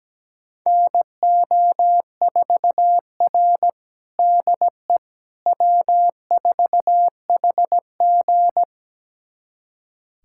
Morse code, heard as a keyed tone, about 17 wpm, 700 Hz. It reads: NO4R DE W4HG
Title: USA Callsign DE Callsign - 4 Characters - 17wpm